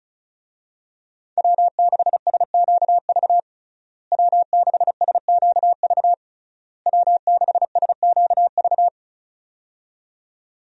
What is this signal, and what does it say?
Morse code, heard as a keyed tone, about 36 wpm, 700 Hz. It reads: W6SQV W6SQV W6SQV
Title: USA Call Signs - 5 Characters - Repeated 3x - 35wpm